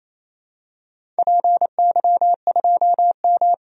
Morse code, heard as a keyed tone, about 28 wpm, 700 Hz. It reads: PY2M
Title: DXCC-100 Call Signs - 4 Characters - 28wpm